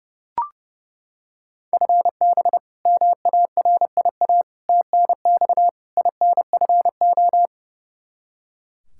Morse code, heard as a keyed tone, about 30 wpm, 700 Hz. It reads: FB MARIA TNX INFO
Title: Practice QSO - Exchange 3 - More Info - 30wpm